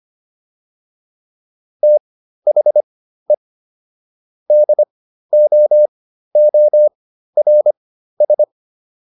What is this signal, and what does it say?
Morse code, heard as a keyed tone, about 25 wpm, spaced as for 13 wpm, 600 Hz. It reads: THE DOORS